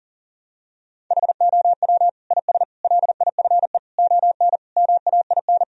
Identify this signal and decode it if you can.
Morse code, heard as a keyed tone, about 37 wpm, 700 Hz. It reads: HOW IS LIFE ON MAIN
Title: Sentences from Top 1000 Words - Review Part 2 - 40wpm